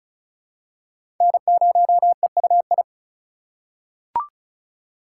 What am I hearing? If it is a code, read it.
Morse code, heard as a keyed tone, about 36 wpm, 700 Hz. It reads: N0EUI
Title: USA Call Signs - 5 Characters - 4x Character Spacing - 35wpm